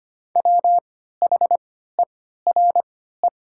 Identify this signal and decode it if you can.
Morse code, heard as a keyed tone, about 25 wpm, spaced as for 14 wpm, 700 Hz. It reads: WHERE
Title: 3x Character Spacing - Top 300 Words - 25wpm